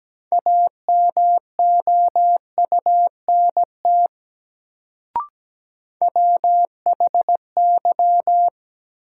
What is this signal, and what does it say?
Morse code, heard as a keyed tone, about 17 wpm, 700 Hz. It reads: AMOUNT WHY
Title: Sets of 3 Words - New Word Drill - 17wpm